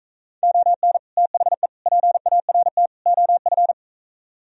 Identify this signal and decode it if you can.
Morse code, heard as a keyed tone, about 45 wpm, 700 Hz. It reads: ON THE PART OF